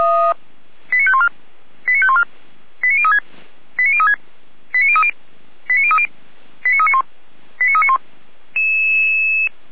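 Five-tone selective call (ZVEI), heard as quick runs of five tones, tones of about 70 ms. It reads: 89514 89514 89036 89036 89030 89030 89391 89391
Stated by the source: ZVEI five-tone selective call received on SDR.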